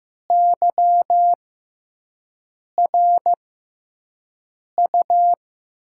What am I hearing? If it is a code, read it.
Morse code, heard as a keyed tone, about 15 wpm, spaced as for 5 wpm, 700 Hz. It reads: YRU
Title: USA Call Signs - 6 Characters - 6x Character Spacing - 15wpm